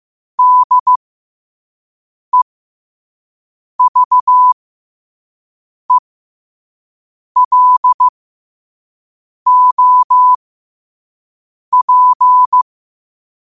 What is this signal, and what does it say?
Morse code, heard as a keyed tone, about 15 wpm, spaced as for 5 wpm, 1000 Hz. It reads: DEVELOP
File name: qotdfiles / 5wpm